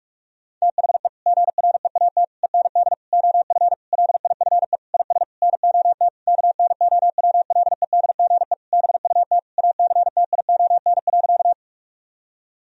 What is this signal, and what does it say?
Morse code, heard as a keyed tone, about 46 wpm, 700 Hz. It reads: THE GREAT END OF LIFE IS NOT KNOWLEDGE BUT ACTION.
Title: Sentences from Top 1000 Words - Review Part 4 - 45wpm